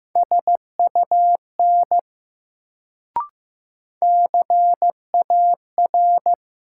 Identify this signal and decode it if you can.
Morse code, heard as a keyed tone, about 16 wpm, 700 Hz. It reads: SUN CAR
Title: Sentences from Top 300 Words - New Word Drill - 15wpm